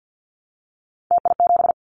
Morse code, heard as a keyed tone, about 50 wpm, 700 Hz. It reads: TI7
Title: Callsign Prefixes for Top 100 DXCC - 50wpm